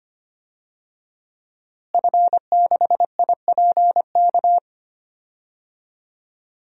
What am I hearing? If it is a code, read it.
Morse code, heard as a keyed tone, about 25 wpm, 700 Hz. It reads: F6IPK
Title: DXCC-100 Call Signs - 5 Characters - Repeated 2x - 25wpm